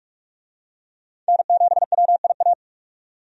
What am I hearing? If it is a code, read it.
Morse code, heard as a keyed tone, about 46 wpm, 700 Hz. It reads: N7WIA
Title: USA Call Signs - 5 Characters - 45wpm